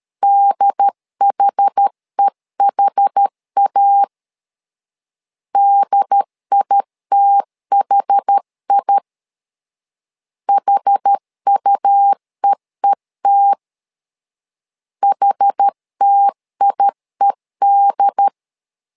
Morse code, heard as a keyed tone, about 13 wpm, 800 Hz. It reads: DHEHA DITHI HUEET HTIED
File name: LEZIONI / 2 UHDN / 2 UHD 12wpm 1540